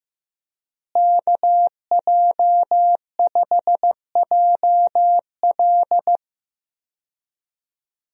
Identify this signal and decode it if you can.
Morse code, heard as a keyed tone, about 15 wpm, 700 Hz. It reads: KJ5JL
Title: USA Call Signs - 5 Characters - Repeated 3x - 15wpm